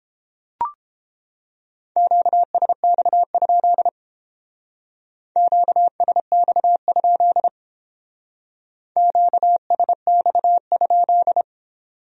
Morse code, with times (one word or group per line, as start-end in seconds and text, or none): 2.0-3.9 QSX?
5.4-7.5 QSX?
9.0-11.4 QSX?